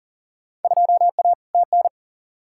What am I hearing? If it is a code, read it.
Morse code, heard as a keyed tone, about 40 wpm, 700 Hz. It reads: 2A TN
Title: Field Day Exchanges Encore - 40wpm